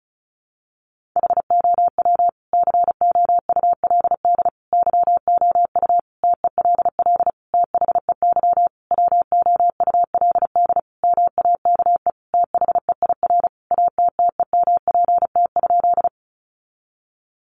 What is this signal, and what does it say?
Morse code, heard as a keyed tone, about 34 wpm, 700 Hz. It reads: HOW COULD YOU TELL THEY WOULD MAKE THEIR ATTEMPT?
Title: Sentences from Top 1000 Words 35wpm